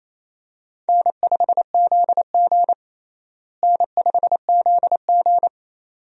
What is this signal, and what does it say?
Morse code, heard as a keyed tone, about 27 wpm, 700 Hz. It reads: N5ZG N5ZG
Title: USA Call Signs - 4 Characters - Repeated 2x - 28wpm